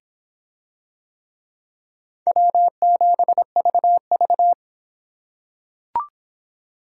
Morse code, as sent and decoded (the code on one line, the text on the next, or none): .-- --... ...- ...-
W7VV